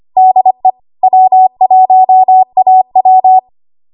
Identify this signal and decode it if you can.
Morse code, heard as a keyed tone, about 25 wpm, 750 Hz. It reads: DE W1AW
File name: Morse / 25WPM